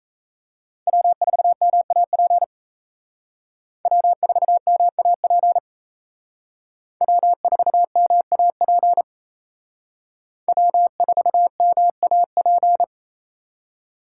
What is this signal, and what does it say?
Morse code, heard as a keyed tone, about 36 wpm, 700 Hz. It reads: W4MAP W4MAP W4MAP W4MAP